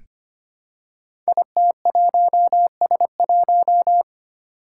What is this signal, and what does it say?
Morse code, heard as a keyed tone, about 25 wpm, 700 Hz. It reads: IT1S1
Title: Five Challenging Letter-Number Combinations 25wpm